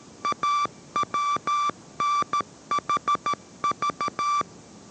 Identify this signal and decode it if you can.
Morse code, heard as a keyed tone, about 13 wpm, 1170 Hz. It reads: AWNHV